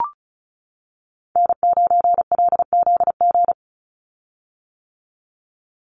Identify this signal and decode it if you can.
Morse code, heard as a keyed tone, about 34 wpm, 700 Hz. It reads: N9LZG
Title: USA Call Signs 35wpm